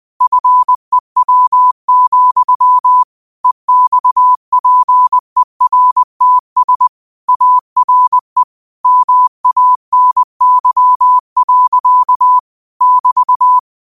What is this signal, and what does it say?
Morse code, heard as a keyed tone, about 20 wpm, 1000 Hz. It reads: FEW, EXPERTS ARE MANY. <BT>
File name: qotdfiles / 20wpm